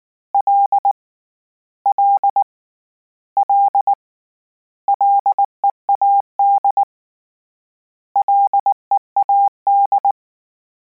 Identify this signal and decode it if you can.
Morse code, heard as a keyed tone, about 19 wpm, 800 Hz. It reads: L L L LEAD LEAD
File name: Combinations / Combinations-mp3s / Combinations@18